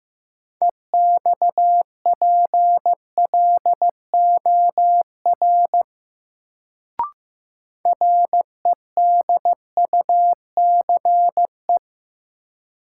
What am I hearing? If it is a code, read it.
Morse code, heard as a keyed tone, about 15 wpm, 700 Hz. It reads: EXPLOR REDUCE
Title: Sentences from Top 1100 Words - New Word Drill - 15wpm